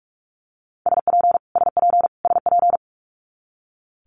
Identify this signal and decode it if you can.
Morse code, heard as a keyed tone, about 46 wpm, 700 Hz. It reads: SP SP SP
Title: Multi-letter Phonemes Repeated 3x 45wpm